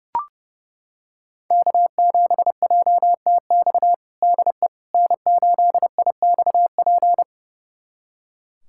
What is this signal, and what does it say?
Morse code, heard as a keyed tone, about 30 wpm, 700 Hz. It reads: K7JTX DE N8IXP
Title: USA Callsign DE Callsign - 5 Characters - 30wpm